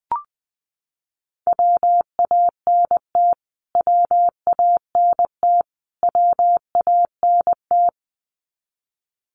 Morse code, heard as a keyed tone, about 20 wpm, 700 Hz. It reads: WANT WANT WANT
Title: Top 200 Words Repeated 20wpm